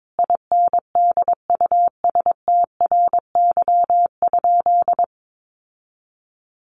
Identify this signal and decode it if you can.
Morse code, heard as a keyed tone, about 22 wpm, 700 Hz. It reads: INDUSTRY?
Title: Sentences from Top 1300 Words 22wpm